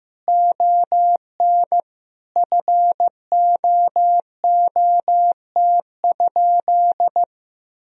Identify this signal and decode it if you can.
Morse code, heard as a keyed tone, about 15 wpm, 700 Hz. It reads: ON FOOT?
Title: Sentences from Top 500 Words and Abbreviations 15wpm